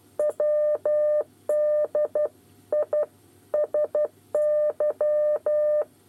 Morse code, heard as a keyed tone, about 12 wpm, 600 Hz. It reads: WDISY
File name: 8 bis lezione cw 2